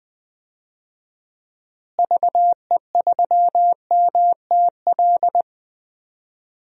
Morse code, heard as a keyed tone, about 20 wpm, 700 Hz. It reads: VE3MTL
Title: DXCC-100 Call Signs - 6 Characters - Repeated 2x - 20wpm